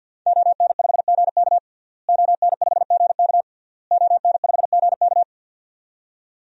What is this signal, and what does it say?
Morse code, heard as a keyed tone, about 53 wpm, 700 Hz. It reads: ON5GK ON5GK ON5GK
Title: DXCC-100 Call Signs - 5 Characters - Repeated 3x - 50wpm